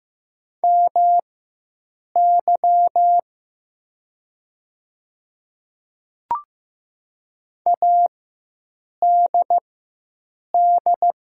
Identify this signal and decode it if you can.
Morse code, heard as a keyed tone, about 15 wpm, spaced as for 7 wpm, 700 Hz. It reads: MY ADD
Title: 4x Character Spacing - Top 300 Words - 15wpm